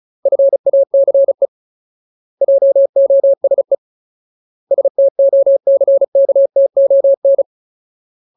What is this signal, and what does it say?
Morse code, heard as a keyed tone, about 35 wpm, 550 Hz. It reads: FACE JOSE STOCKTON
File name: Ultimate Word List / 35WPM Ultimate